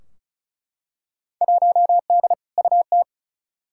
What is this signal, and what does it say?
Morse code, heard as a keyed tone, about 36 wpm, 700 Hz. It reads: 1D UT
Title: Field Day Exchanges - 35wpm